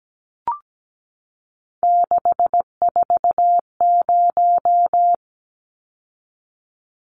Morse code, heard as a keyed tone, about 17 wpm, 700 Hz. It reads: 640
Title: Numbers - 3-Digit 17wpm